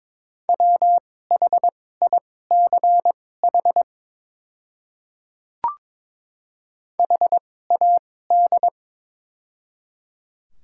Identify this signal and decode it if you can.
Morse code, heard as a keyed tone, about 22 wpm, spaced as for 16 wpm, 700 Hz. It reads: WHICH HAD
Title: 2x Character Spacing - Top 300 Words - 22wpm